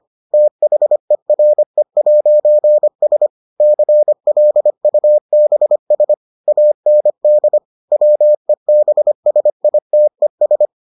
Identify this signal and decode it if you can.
Morse code, heard as a keyed tone, about 25 wpm, 600 Hz. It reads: THERE'S CLUBS AND WEBSITES